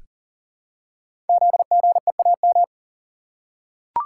Morse code, heard as a keyed tone, about 38 wpm, 690 Hz. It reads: ZGEAM
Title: Code Groups - 5 Letters - 40wpm